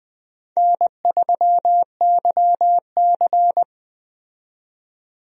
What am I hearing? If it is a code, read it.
Morse code, heard as a keyed tone, about 21 wpm, 700 Hz. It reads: N3YC
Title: USA Call Signs - 4 Characters - 20wpm